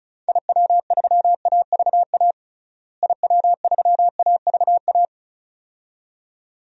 Morse code, heard as a keyed tone, about 36 wpm, 700 Hz. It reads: IW3AVA IW3AVA
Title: DXCC-100 Call Signs - 6 Characters - Repeated 2x - 35wpm